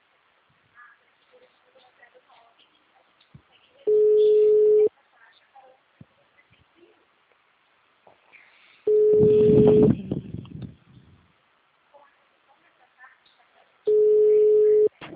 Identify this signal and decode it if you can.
Ringback tone (Europe, 425 Hz).